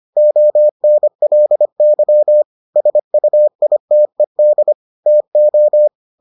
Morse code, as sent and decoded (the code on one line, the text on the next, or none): --- -. .-.. -.-- / ... ..- .. - . -.. / - ---
ONLY SUITED TO